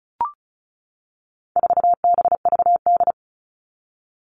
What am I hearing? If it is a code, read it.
Morse code, heard as a keyed tone, about 34 wpm, 700 Hz. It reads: 4BVD
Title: All Cut Numbers - 4-Digit 35wpm